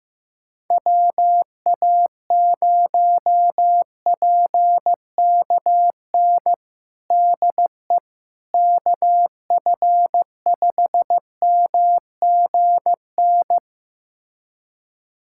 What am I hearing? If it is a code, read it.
Morse code, heard as a keyed tone, about 15 wpm, 700 Hz. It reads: WA0PKN DE KF5MGN
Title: USA Callsign DE Callsign - 6 Characters - 15wpm